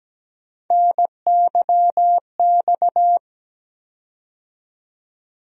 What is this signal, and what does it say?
Morse code, heard as a keyed tone, about 17 wpm, 700 Hz. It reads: NYX
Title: USA Call Sign Postfixes - 3 Letters 17wpm